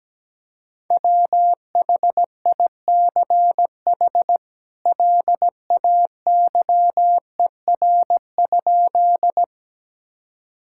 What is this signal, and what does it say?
Morse code, heard as a keyed tone, about 17 wpm, 700 Hz. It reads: WHICH LAYER?
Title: Sentences from Top 1600 Words 17wpm